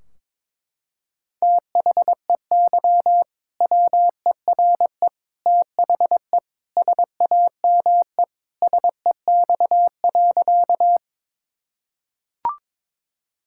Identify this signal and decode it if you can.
Morse code, heard as a keyed tone, about 22 wpm, 700 Hz. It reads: THEY WERE THE SAME SEX.
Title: Sentences from Top 1600 Words 22wpm